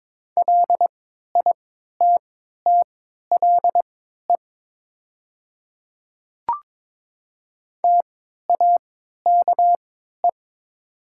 Morse code, heard as a keyed tone, about 22 wpm, spaced as for 12 wpm, 700 Hz. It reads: LITTLE TAKE
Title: 3x Character Spacing - Top 300 Words - 22wpm